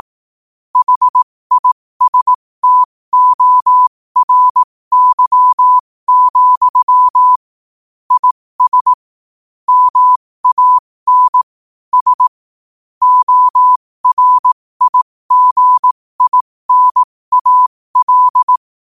Morse code, heard as a keyed tone, about 18 wpm, spaced as for 15 wpm, 1000 Hz. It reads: HISTORY, IS MAN S ORIGINAL
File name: qotdfiles / 15wpm